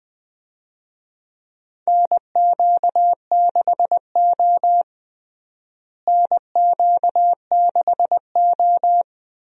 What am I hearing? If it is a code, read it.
Morse code, heard as a keyed tone, about 20 wpm, 700 Hz. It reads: NQ6O NQ6O
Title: USA Call Signs - 4 Characters - Repeated 3x - 20wpm